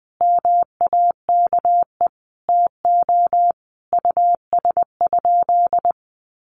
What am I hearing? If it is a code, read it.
Morse code, heard as a keyed tone, about 20 wpm, 700 Hz. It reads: MAKE TO US?
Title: Sentences from Top 1400 Words 20wpm